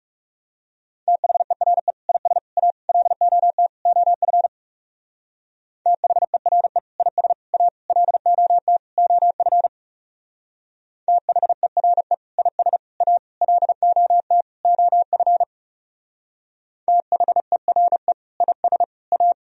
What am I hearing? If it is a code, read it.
Morse code, heard as a keyed tone, about 36 wpm, 700 Hz. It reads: THERE IS A LOT OF THERE IS A LOT OF THERE IS A LOT OF THERE IS A
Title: Speed-Racing - Sets of 5 Words 30wpm